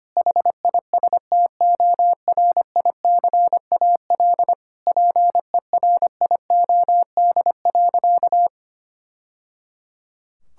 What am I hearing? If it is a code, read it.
Morse code, heard as a keyed tone, about 25 wpm, 700 Hz. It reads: HISTORICAL PERIOD.
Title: Sentences from Top 1000 Words - Review Part 5 - 25wpm